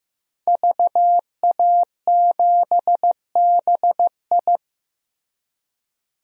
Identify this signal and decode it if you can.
Morse code, heard as a keyed tone, about 16 wpm, 700 Hz. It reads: VA7BI
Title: DXCC-100 Call Signs - 5 Characters - Repeated 2x - 15wpm